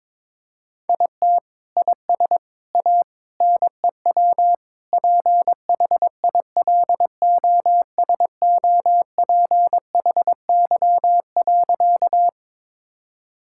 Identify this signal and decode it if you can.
Morse code, heard as a keyed tone, about 21 wpm, 700 Hz. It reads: IT IS A NEW PHILOSOPHY.
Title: Sentences from Top 1000 Words - Review Part 6 - 22wpm